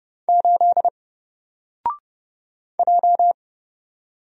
Morse code, heard as a keyed tone, about 30 wpm, 700 Hz. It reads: 8 J
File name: Single-Letter-Number-30wpm